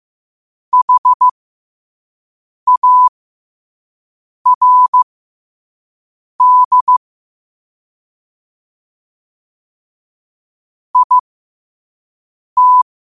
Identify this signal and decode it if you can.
Morse code, heard as a keyed tone, about 15 wpm, spaced as for 5 wpm, 1000 Hz. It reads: HARD IT
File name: qotdfiles / 5wpm